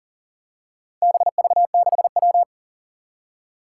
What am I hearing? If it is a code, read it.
Morse code, heard as a keyed tone, about 38 wpm, 700 Hz. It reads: BV6W
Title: Four Challenging Letter-Number Combinations 40wpm